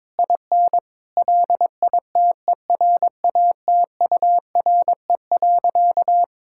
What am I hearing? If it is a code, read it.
Morse code, heard as a keyed tone, about 22 wpm, 700 Hz. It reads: IN LITERATURE.
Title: Sentences from Top 1000 Words - Review Part 5 - 22wpm